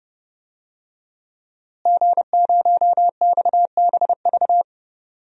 Morse code, heard as a keyed tone, about 30 wpm, 700 Hz. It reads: G0XBV
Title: DXCC-100 Call Signs - 5 Characters - Repeated 3x - 30wpm